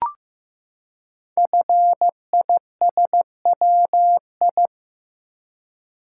Morse code, heard as a keyed tone, about 15 wpm, 700 Hz. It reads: FISWI
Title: Code Groups - 5 Letters - 15wpm